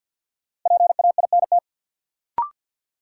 Morse code, heard as a keyed tone, about 52 wpm, 700 Hz. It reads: PAINT
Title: Sentences from Top 1000 Words - Review Part 6 - New Word Drill - 50wpm